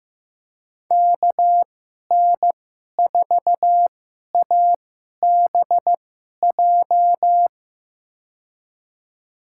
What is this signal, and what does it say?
Morse code, heard as a keyed tone, about 16 wpm, spaced as for 11 wpm, 700 Hz. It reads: KN4ABJ
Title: USA Call Signs - 6 Characters - 2x Character Spacing - 15wpm